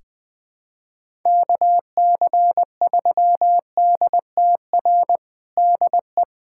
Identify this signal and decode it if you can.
Morse code, heard as a keyed tone, about 20 wpm, 700 Hz. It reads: KC3DTR DE
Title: USA Callsign DE Callsign 20wpm